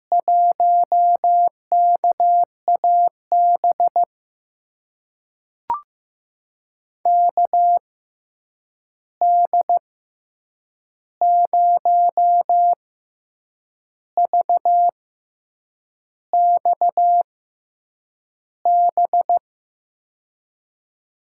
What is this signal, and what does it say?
Morse code, heard as a keyed tone, about 16 wpm, spaced as for 7 wpm, 700 Hz. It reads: 1KAB KD0VXB
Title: USA Call Signs - 6 Characters - 6x Character Spacing - 15wpm